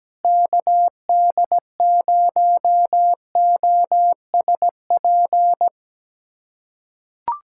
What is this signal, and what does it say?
Morse code, heard as a keyed tone, about 17 wpm, 700 Hz. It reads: KD0OSP
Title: USA Callsign DE Callsign 17wpm